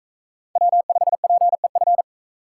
Morse code, heard as a keyed tone, about 42 wpm, 700 Hz. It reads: W5PEF